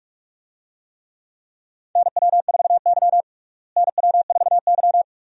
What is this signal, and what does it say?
Morse code, heard as a keyed tone, about 45 wpm, 700 Hz. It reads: NW4Y NW4Y
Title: USA Call Signs - 4 Characters - Repeated 2x - 45wpm